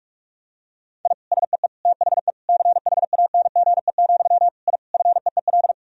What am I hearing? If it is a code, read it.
Morse code, heard as a keyed tone, about 46 wpm, 700 Hz. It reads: I SEE THE CHANGE, I FEEL
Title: Sentences from Top 1000 Words - Review Part 1 - 45wpm